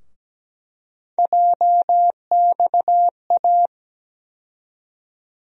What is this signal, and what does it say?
Morse code, heard as a keyed tone, about 17 wpm, 700 Hz. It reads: JXA
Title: USA Call Sign Postfixes - 3 Letters 17wpm